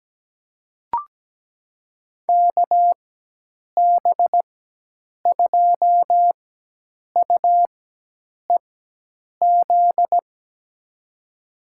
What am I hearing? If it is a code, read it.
Morse code, heard as a keyed tone, about 17 wpm, spaced as for 8 wpm, 700 Hz. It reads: KB2UEZ
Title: USA Call Signs - 6 Characters - 4x Character Spacing - 17wpm